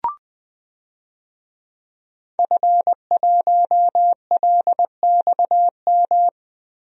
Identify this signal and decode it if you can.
Morse code, heard as a keyed tone, about 21 wpm, 700 Hz. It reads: F1LXM
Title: DXCC-100 Call Signs - 5 Characters - Repeated 2x - 20wpm